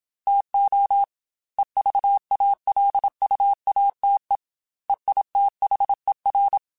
Morse code, heard as a keyed tone, about 27 wpm, 800 Hz. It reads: TO EVALUATE EITHER
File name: Technical Subjects / Technical Subjects-mp3s / AND Gate @25